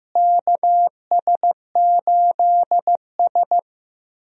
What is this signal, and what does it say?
Morse code, heard as a keyed tone, about 15 wpm, 700 Hz. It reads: KS8S
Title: USA Call Signs - 4 Characters - Repeated 3x - 15wpm